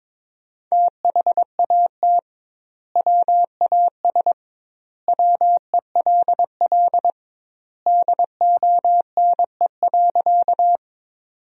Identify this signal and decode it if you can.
Morse code, heard as a keyed tone, about 21 wpm, 700 Hz. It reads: THAT WAS WELL DONE.
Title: 2x Word-Spacing - Sentences from Top 500 Words - 22wpm